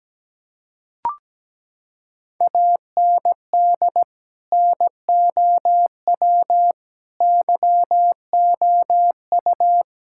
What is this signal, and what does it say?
Morse code, heard as a keyed tone, about 17 wpm, 700 Hz. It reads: AND NOW YOU
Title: Sentences from Top 1000 Words - Review Part 6 - 17wpm